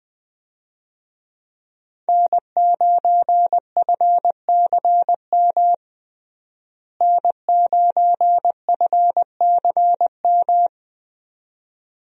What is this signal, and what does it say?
Morse code, heard as a keyed tone, about 20 wpm, 700 Hz. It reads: N9FCM N9FCM
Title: USA Call Signs - 5 Characters - Repeated 2x - 20wpm